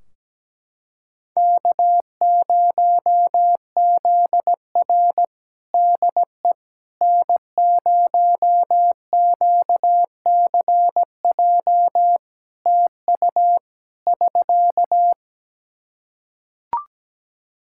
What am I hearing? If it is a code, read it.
Morse code, heard as a keyed tone, about 17 wpm, 700 Hz. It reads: K0ZR DE N0QCJ TU <SK>
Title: Practice QSO - Exchange 4 - Ending - 17wpm